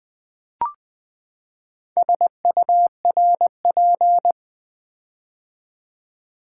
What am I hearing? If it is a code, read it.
Morse code, heard as a keyed tone, about 20 wpm, 700 Hz. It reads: SURP
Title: Build A Word - 4 Letters At A Time - 8-Letter Words - 20wpm